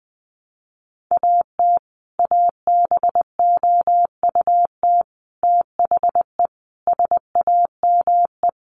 Morse code, heard as a keyed tone, about 20 wpm, 700 Hz. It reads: AT ABOUT THE SAME
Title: Sets of 5 Words Encore 20wpm